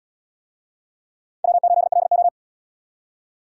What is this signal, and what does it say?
Morse code, heard as a keyed tone, about 73 wpm, 700 Hz. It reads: W8LO